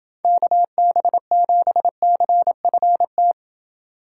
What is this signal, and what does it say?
Morse code, heard as a keyed tone, about 29 wpm, 700 Hz. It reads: KB7CFT